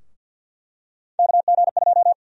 Morse code, heard as a keyed tone, about 47 wpm, 700 Hz. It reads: KG2
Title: USA Call Sign Prefixes - Repeated 2x - 50wpm